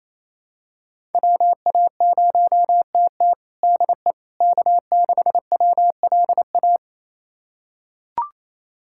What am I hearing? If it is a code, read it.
Morse code, heard as a keyed tone, about 28 wpm, 700 Hz. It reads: WA0TT DE K6WLA
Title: USA Callsign DE Callsign - 5 Characters - 28wpm